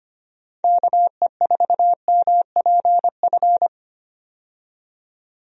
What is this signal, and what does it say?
Morse code, heard as a keyed tone, about 25 wpm, 700 Hz. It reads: KE4MPF
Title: USA Call Signs - 6 Characters - 25wpm